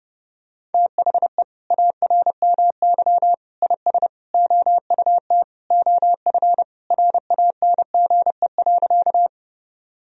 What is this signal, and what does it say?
Morse code, heard as a keyed tone, about 30 wpm, 700 Hz. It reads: THE ARMY IS OUT OF RANGE.
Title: Sentences from Top 1000 Words - Review Part 4 - 30wpm